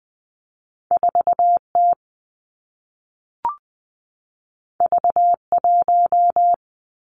Morse code, heard as a keyed tone, about 20 wpm, 700 Hz. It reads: 4 T V 1
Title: Two Challenging Letter-Number Combinations 20wpm